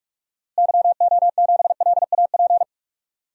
Morse code, heard as a keyed tone, about 46 wpm, 700 Hz. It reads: YO7LAP
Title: DXCC-100 Call Signs - 6 Characters - Repeated 3x - 45wpm